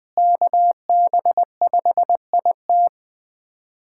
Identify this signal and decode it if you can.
Morse code, heard as a keyed tone, about 19 wpm, 700 Hz. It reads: KB5IT
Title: USA Callsign DE Callsign - 5 Characters - 20wpm